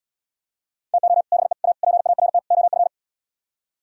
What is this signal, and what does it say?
Morse code, heard as a keyed tone, about 77 wpm, 700 Hz. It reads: TO BE A PART OF